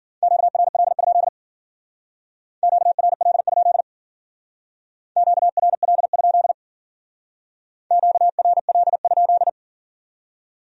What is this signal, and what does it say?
Morse code, heard as a keyed tone, about 52 wpm, 700 Hz. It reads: QRL? QRL? QRL? QRL?